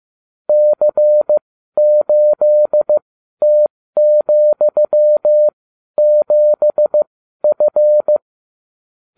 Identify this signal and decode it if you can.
Morse code, heard as a keyed tone, about 15 wpm, spaced as for 12 wpm, 600 Hz. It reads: C8T,7F